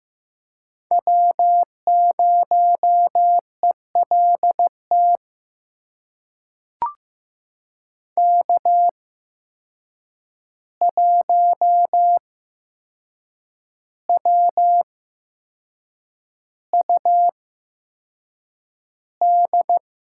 Morse code, heard as a keyed tone, about 15 wpm, spaced as for 6 wpm, 700 Hz. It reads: W0ELT K1WUD
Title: USA Call Signs - 5 Characters - 8x Character Spacing - 15wpm